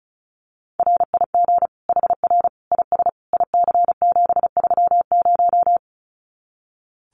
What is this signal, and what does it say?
Morse code, heard as a keyed tone, about 34 wpm, 700 Hz. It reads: RIG HR IS IC730
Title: Common QSO Phrases 35wpm